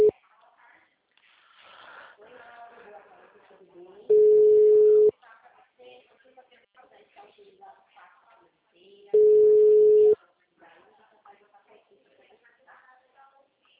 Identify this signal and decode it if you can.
Ringback tone (Europe, 425 Hz), with speech.
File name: out- -2036- - - .59582